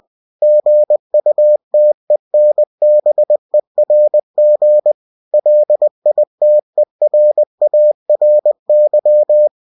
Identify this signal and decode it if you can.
Morse code, heard as a keyed tone, about 20 wpm, 600 Hz. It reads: GUTENBERG LITERARY